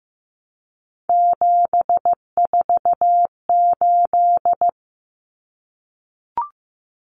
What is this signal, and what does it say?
Morse code, heard as a keyed tone, about 15 wpm, 700 Hz. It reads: 748
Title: Common Cut Numbers - 3-Digit 15wpm